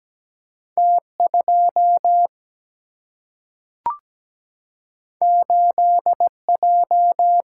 Morse code, heard as a keyed tone, about 17 wpm, 700 Hz. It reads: T 2 8 J
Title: Single-Letter Single-Number 17wpm